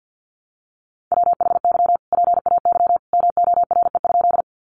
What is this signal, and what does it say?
Morse code, heard as a keyed tone, about 50 wpm, 700 Hz. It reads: WHY PAY MORE?
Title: Sentences from Top 700 Words 50wpm